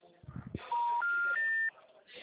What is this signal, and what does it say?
Special information tone.
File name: out- -2005- - - .50